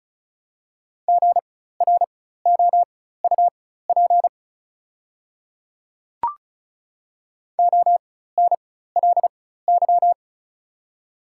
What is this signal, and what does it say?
Morse code, heard as a keyed tone, about 34 wpm, spaced as for 16 wpm, 690 Hz. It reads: GROUP ONLY
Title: 4x Character Spacing - Top 500 Words - 35wpm